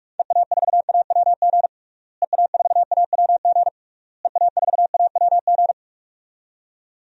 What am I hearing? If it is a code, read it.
Morse code, heard as a keyed tone, about 46 wpm, 700 Hz. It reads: EA4AWG EA4AWG EA4AWG
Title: DXCC-100 Call Signs - 6 Characters - Repeated 3x - 45wpm